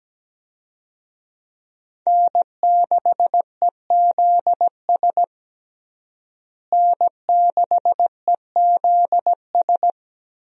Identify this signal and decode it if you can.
Morse code, heard as a keyed tone, about 17 wpm, 700 Hz. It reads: N6EZS N6EZS
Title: USA Call Signs - 5 Characters - Repeated 2x - 17wpm